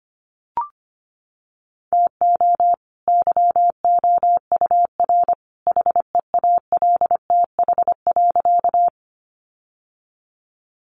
Morse code, heard as a keyed tone, about 25 wpm, 700 Hz. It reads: TO YOUR HEALTH.
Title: Sentences from Top 600 Words 25wpm